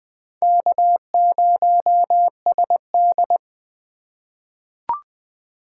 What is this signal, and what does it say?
Morse code, heard as a keyed tone, about 21 wpm, 700 Hz. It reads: K0SD
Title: USA Call Signs - 4 Characters - 3x Character Spacing - 20wpm